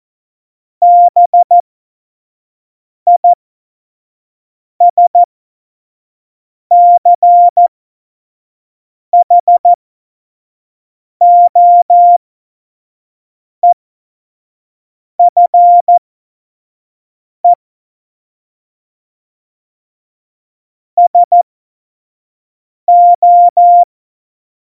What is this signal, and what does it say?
Morse code, heard as a keyed tone, about 14 wpm, spaced as for 5 wpm, 700 Hz. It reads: BISCHOEFE SO
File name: Morsekurs / Witze / Witz 12. Januar 2021, 25 F70